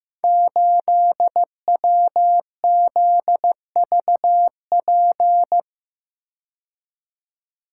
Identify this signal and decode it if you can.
Morse code, heard as a keyed tone, about 15 wpm, 700 Hz. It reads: 8WZVP
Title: Code Groups - 5 Letter:Numbers - 15wpm